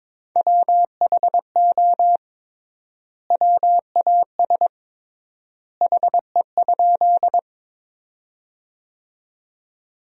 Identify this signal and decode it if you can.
Morse code, heard as a keyed tone, about 22 wpm, 700 Hz. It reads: WHO WAS HE?
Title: 3x Word-Spacing - Sentences from Top 100 Words - 22wpm